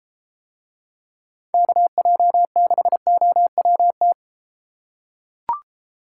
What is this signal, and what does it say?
Morse code, heard as a keyed tone, about 33 wpm, 700 Hz. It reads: KJ6OWT